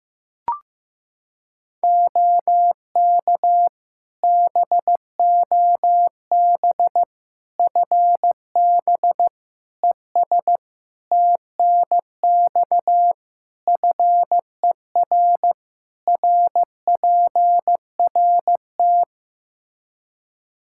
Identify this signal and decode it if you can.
Morse code, heard as a keyed tone, about 16 wpm, 700 Hz. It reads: OK BOB FB ES TNX FER RPRT
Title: Practice QSO - Exchange 2 - Info - 15wpm